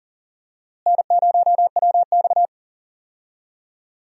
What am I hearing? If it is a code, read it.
Morse code, heard as a keyed tone, about 40 wpm, 700 Hz. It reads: N0WX
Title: USA Call Signs - 4 Characters - Repeated 3x - 40wpm